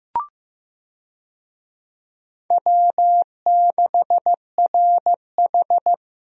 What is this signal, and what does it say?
Morse code, heard as a keyed tone, about 16 wpm, 700 Hz. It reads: W6RH
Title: USA Call Signs - 4 Characters - Repeated 3x - 15wpm